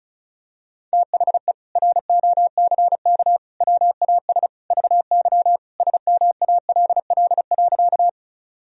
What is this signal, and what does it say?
Morse code, heard as a keyed tone, about 34 wpm, 700 Hz. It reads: THE ROCK WAS VY SMALL.
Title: Sentences from Top 400 Words and Abbreviations 35wpm